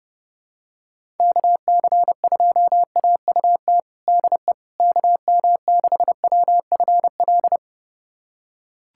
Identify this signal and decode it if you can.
Morse code, heard as a keyed tone, about 30 wpm, 700 Hz. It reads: KC2AUT DE KM6WFL
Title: USA Callsign DE Callsign - 6 Characters - 30wpm